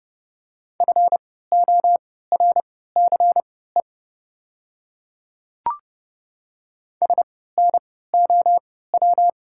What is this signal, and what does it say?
Morse code, heard as a keyed tone, about 30 wpm, spaced as for 17 wpm, 690 Hz. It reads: FORCE SNOW
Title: 3x Character Spacing - Top 500 Words - 30wpm